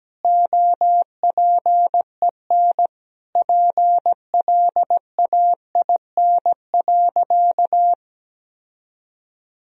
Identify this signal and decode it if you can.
Morse code, heard as a keyed tone, about 17 wpm, 700 Hz. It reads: OPEN PLAIN.
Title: Sentences from Top 1000 Words - Review Part 2 - 17wpm